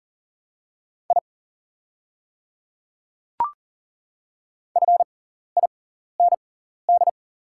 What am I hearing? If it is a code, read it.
Morse code, heard as a keyed tone, about 40 wpm, spaced as for 14 wpm, 700 Hz. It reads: I FIND
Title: 6x Character Spacing - Top 100 Words - 40wpm